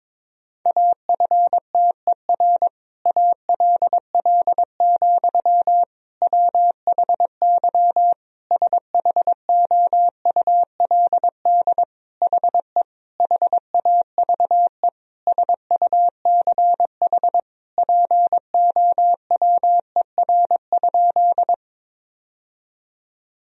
Morse code, heard as a keyed tone, about 22 wpm, 700 Hz. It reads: AFTER ALL, WHY SHOULD HE HAVE SUCH POWER?
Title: Sentences from Top 1000 Words - Review Part 2 - 22wpm